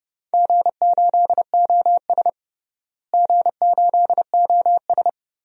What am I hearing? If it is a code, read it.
Morse code, heard as a keyed tone, about 30 wpm, 700 Hz. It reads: G8OS G8OS
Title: DXCC-100 Call Signs - 4 Characters - Repeated 2x - 30wpm